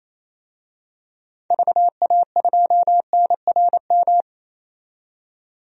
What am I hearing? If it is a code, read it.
Morse code, heard as a keyed tone, about 28 wpm, 700 Hz. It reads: VA2NRM
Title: DXCC-100 Call Signs - 6 Characters - 28wpm